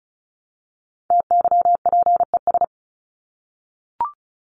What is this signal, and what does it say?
Morse code, heard as a keyed tone, about 34 wpm, 700 Hz. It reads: TYPES
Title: ES:S Suffix - Top 100 Words with Base Words 35wpm